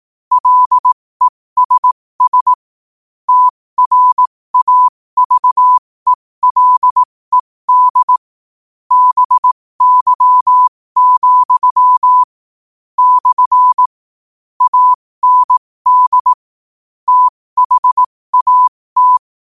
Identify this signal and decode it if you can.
Morse code, heard as a keyed tone, about 18 wpm, spaced as for 15 wpm, 1000 Hz. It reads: LESS TRAVELED BY, / AND THAT
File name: qotdfiles / 15wpm